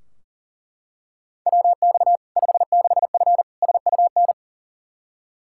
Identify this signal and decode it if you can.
Morse code, heard as a keyed tone, about 40 wpm, 700 Hz. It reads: WX 56F SUN
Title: Practice QSO - Exchange 2 - Info - 40wpm